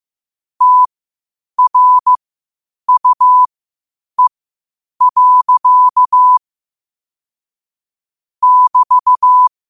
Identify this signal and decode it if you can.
Morse code, heard as a keyed tone, about 15 wpm, spaced as for 8 wpm, 1000 Hz. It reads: TRUE. <BT>
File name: qotdfiles / 8wpm